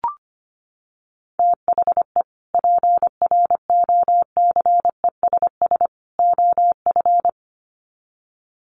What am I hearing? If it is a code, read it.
Morse code, heard as a keyed tone, about 25 wpm, 700 Hz. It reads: THE PROCESS OF
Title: Sets of 3 Words 25wpm